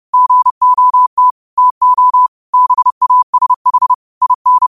Morse code, heard as a keyed tone, about 30 wpm, 1000 Hz. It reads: GOT TO BASH IN
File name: qotdfiles / 30wpm